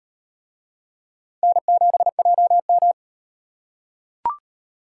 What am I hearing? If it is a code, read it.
Morse code, heard as a keyed tone, about 38 wpm, 700 Hz. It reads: N7JM